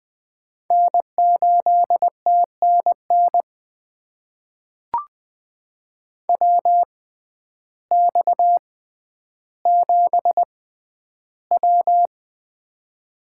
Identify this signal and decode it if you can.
Morse code, heard as a keyed tone, about 20 wpm, spaced as for 11 wpm, 700 Hz. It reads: N8TNN WX7W
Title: USA Call Signs - 6x Character Spacing - 20wpm